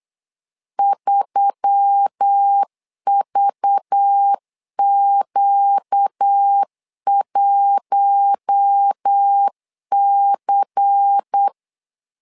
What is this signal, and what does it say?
Morse code, heard as a keyed tone, about 8 wpm, 800 Hz. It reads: 3VQ1C
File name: LEZIONI / 5 LBVC / 5 LBVC 8wpm 2935